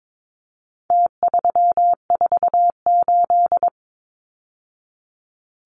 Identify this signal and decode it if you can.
Morse code, heard as a keyed tone, about 22 wpm, 700 Hz. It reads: T348
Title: Numbers - Contest Style - 4-Digit 22wpm